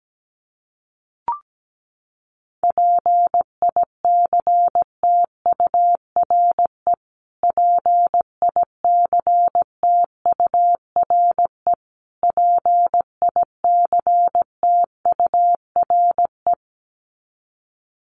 Morse code, heard as a keyed tone, about 17 wpm, 700 Hz. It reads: PICTURE PICTURE PICTURE
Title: Top 200 Words Repeated 17wpm